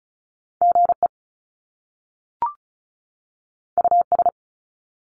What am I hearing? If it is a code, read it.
Morse code, heard as a keyed tone, about 36 wpm, 700 Hz. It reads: GE US
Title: Top Two Letter Combinations in Words 35wpm